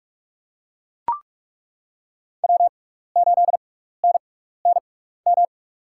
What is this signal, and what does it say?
Morse code, heard as a keyed tone, about 43 wpm, spaced as for 15 wpm, 700 Hz. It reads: W8NNM
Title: USA Call Signs - 5 Characters - 6x Character Spacing - 45wpm